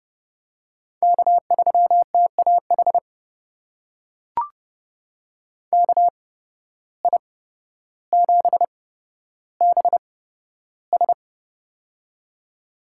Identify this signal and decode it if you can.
Morse code, heard as a keyed tone, about 30 wpm, spaced as for 13 wpm, 700 Hz. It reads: K3TAH KI7BS
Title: USA Call Signs - 5 Characters - 8x Character Spacing - 30wpm